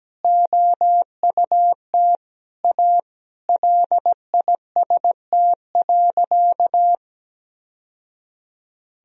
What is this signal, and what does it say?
Morse code, heard as a keyed tone, about 17 wpm, 700 Hz. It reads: OUT A LIST.
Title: Sentences from Top 1000 Words - Review Part 2 - 17wpm